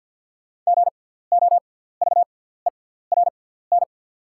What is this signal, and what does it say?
Morse code, heard as a keyed tone, about 48 wpm, spaced as for 17 wpm, 690 Hz. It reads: GOVERN
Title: 6x Character Spacing - Top 500 Words - 50wpm